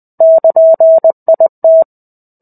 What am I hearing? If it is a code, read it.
Morse code, heard as a keyed tone, about 20 wpm, 650 Hz. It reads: <KN>IT
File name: 05-Chapter-05 20 WPM